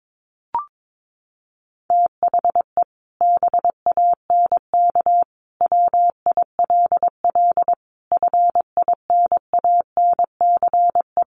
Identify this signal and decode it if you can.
Morse code, heard as a keyed tone, about 22 wpm, 700 Hz. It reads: THE BANK WILL FINANCE
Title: Sentences from Top 1500 Words 22wpm